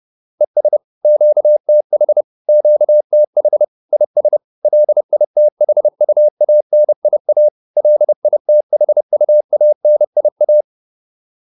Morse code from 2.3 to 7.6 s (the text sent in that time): QTH IS LITHUANIA